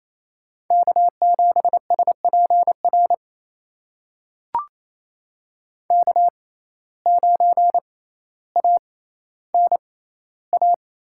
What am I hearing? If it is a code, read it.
Morse code, heard as a keyed tone, about 28 wpm, spaced as for 14 wpm, 700 Hz. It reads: K7SPR K9ANA
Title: USA Call Signs - 5 Characters - 6x Character Spacing - 28wpm